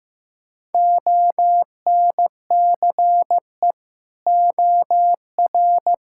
Morse code, heard as a keyed tone, about 15 wpm, 700 Hz. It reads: ONCE OR
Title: Sentences from Top 1600 Words 15wpm